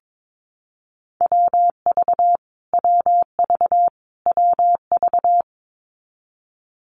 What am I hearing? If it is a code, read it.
Morse code, heard as a keyed tone, about 22 wpm, 700 Hz. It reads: WV WV WV
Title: US States Repeated 3x 22wpm